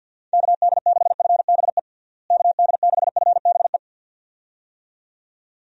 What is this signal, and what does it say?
Morse code, heard as a keyed tone, about 51 wpm, 700 Hz. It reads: KD6FBE KD6FBE
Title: USA Call Signs - 6 Characters - Repeated 2x - 50wpm